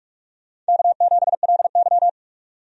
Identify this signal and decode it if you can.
Morse code, heard as a keyed tone, about 45 wpm, 700 Hz. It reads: K7LY
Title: USA Call Signs - 4 Characters - 45wpm